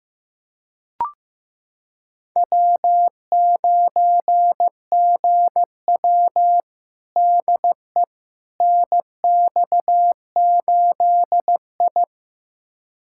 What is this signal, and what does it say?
Morse code, heard as a keyed tone, about 15 wpm, 700 Hz. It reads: W9GW DE NX8I
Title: USA Callsign DE Callsign - 4 Characters - 15wpm